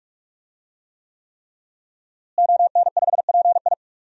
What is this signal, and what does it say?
Morse code, heard as a keyed tone, about 46 wpm, 700 Hz. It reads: ON5PI
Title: DXCC-100 Call Signs - 5 Characters - Repeated 3x - 45wpm